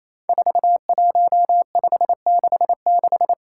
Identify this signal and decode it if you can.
Morse code, heard as a keyed tone, about 29 wpm, 700 Hz. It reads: 41566
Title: Five Challenging Letter-Number Combinations 28wpm